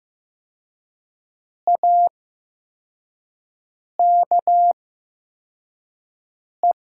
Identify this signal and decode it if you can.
Morse code, heard as a keyed tone, about 15 wpm, spaced as for 4 wpm, 700 Hz. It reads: AKE
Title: 8x Character Spacing - Top 200 Words - 15wpm